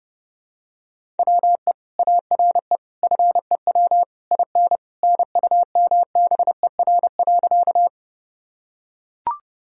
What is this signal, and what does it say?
Morse code, heard as a keyed tone, about 30 wpm, 700 Hz. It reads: WE ARE FEW IN NUMBER.
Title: Sentences from Top 1000 Words - Review Part 1 - 30wpm